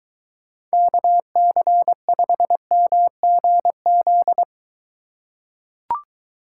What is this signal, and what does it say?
Morse code, heard as a keyed tone, about 22 wpm, 700 Hz. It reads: KC5MGZ